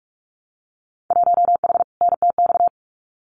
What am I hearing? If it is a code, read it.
Morse code, heard as a keyed tone, about 46 wpm, 700 Hz. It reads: 1H NTX
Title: Winter Field Day Exchanges 45wpm